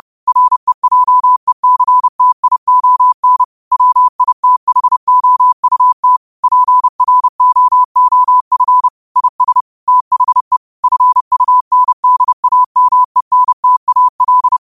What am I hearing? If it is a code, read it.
Morse code, heard as a keyed tone, about 30 wpm, 1000 Hz. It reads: REJECTION WITHOUT PROOF IS THE FUNDAMENTAL